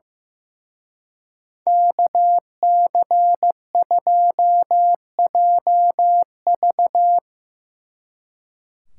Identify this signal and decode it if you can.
Morse code, heard as a keyed tone, about 16 wpm, 700 Hz. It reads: KC2JV
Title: Speed Racing - USA 2x2 Call Signs 15wpm